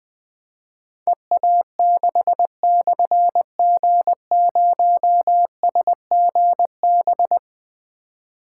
Morse code, heard as a keyed tone, about 20 wpm, 700 Hz. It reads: EA6/G0SGB
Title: Difficult Call Signs 20wpm